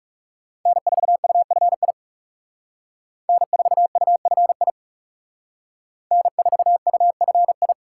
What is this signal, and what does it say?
Morse code, heard as a keyed tone, about 36 wpm, 700 Hz. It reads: N4UFI N4UFI N4UFI
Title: Speed Racing - USA 1x3 Call Signs 30wpm